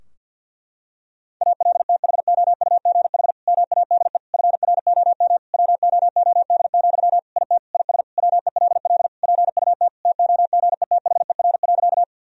Morse code, heard as a keyed tone, about 51 wpm, 700 Hz. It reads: ALTHOUGH MADE FROM WOOD, IT IS WELL PUT TOGETHER.
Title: Sentences from Top 1000 Words - Review Part 3 - 50wpm